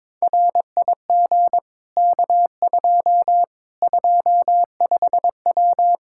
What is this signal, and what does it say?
Morse code, heard as a keyed tone, about 22 wpm, 700 Hz. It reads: RIG K2 25W
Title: Practice QSO - Exchange 2 - Info - 22wpm